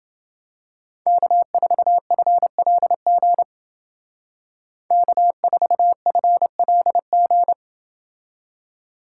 Morse code, heard as a keyed tone, about 28 wpm, 700 Hz. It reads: K4FLG K4FLG